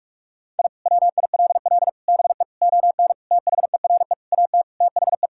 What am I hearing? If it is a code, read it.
Morse code, heard as a keyed tone, about 46 wpm, 700 Hz. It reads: I WILL BE ON THERE AT THE
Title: Sentences from Top 1000 Words - Review Part 5 - 45wpm